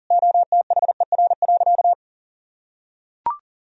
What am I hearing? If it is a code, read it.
Morse code, heard as a keyed tone, about 40 wpm, 700 Hz. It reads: OTHER.
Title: Sentences from Top 1000 Words - Review Part 1 - 40wpm